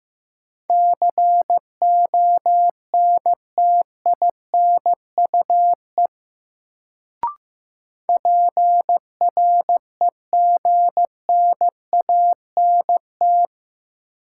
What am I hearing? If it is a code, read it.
Morse code, heard as a keyed tone, about 15 wpm, 700 Hz. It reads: CONTINUE PREGNANT
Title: Sentences from Top 1000 Words - Review Part 6 - New Word Drill - 15wpm